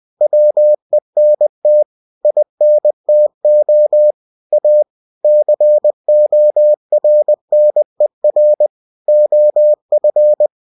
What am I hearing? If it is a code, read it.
Morse code, heard as a keyed tone, about 20 wpm, 600 Hz. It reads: WENT INTO A CORNER OF
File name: cw-books / winnie-the-pooh / winnie-20wpm-page-0063